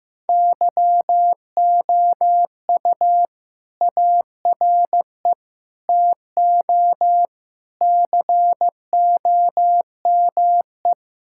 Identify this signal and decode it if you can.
Morse code, heard as a keyed tone, about 15 wpm, 700 Hz. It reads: YOU ARE TO COME
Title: Sentences from Top 1600 Words 15wpm